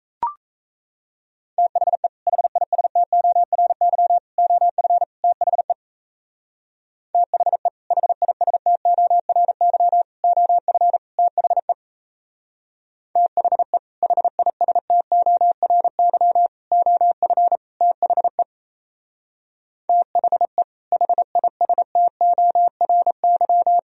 Morse code, heard as a keyed tone, about 33 wpm, 700 Hz. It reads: THE HISTORY OF THE THE HISTORY OF THE THE HISTORY OF THE THE HISTORY
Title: Speed-Racing - Sets of 4 Words 28wpm